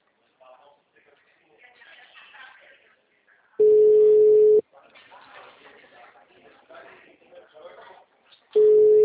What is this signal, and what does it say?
Ringback tone (Europe, 425 Hz), with speech.
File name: out- -2004- - - .74958